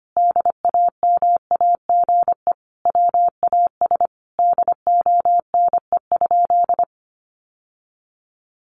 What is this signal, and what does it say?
Morse code, heard as a keyed tone, about 25 wpm, 700 Hz. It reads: DAMAGE WAS DONE?
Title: Sentences from Top 1100 Words 25wpm